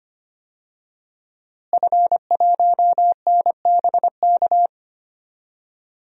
Morse code, heard as a keyed tone, about 25 wpm, 700 Hz. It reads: F1NBK
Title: DXCC-100 Call Signs - 5 Characters - Repeated 2x - 25wpm